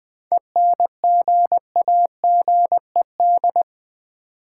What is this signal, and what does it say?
Morse code, heard as a keyed tone, about 20 wpm, 700 Hz. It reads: ENGAGED
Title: 7 Letter Words - First 3 Letters Given - 20wpm